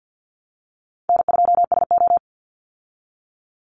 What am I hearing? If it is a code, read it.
Morse code, heard as a keyed tone, about 49 wpm, 700 Hz. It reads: N2SO
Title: Call Signs Repeated 2x - 50wpm